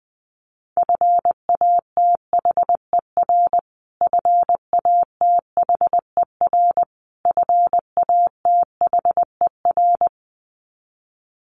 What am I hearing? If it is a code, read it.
Morse code, heard as a keyed tone, about 20 wpm, 700 Hz. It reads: FATHER FATHER FATHER
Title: Top 300 Words Repeated 20wpm